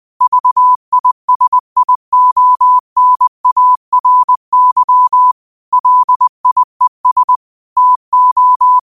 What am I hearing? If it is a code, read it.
Morse code, heard as a keyed tone, about 20 wpm, 1000 Hz. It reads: VISIONARY LIES TO
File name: qotdfiles / 20wpm